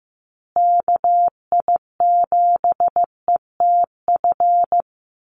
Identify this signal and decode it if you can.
Morse code, heard as a keyed tone, about 15 wpm, 700 Hz. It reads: KI7ETF
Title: USA Call Signs Repeated 3x 15wpm